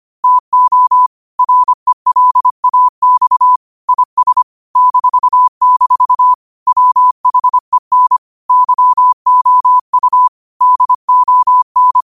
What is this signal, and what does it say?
Morse code, heard as a keyed tone, about 25 wpm, 1000 Hz. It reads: TO RELAX IS -- WHEN YOU DON